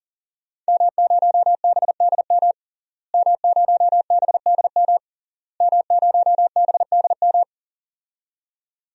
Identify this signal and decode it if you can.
Morse code, heard as a keyed tone, about 40 wpm, 700 Hz. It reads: M0BDM M0BDM M0BDM
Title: DXCC-100 Call Signs - 5 Characters - Repeated 3x - 40wpm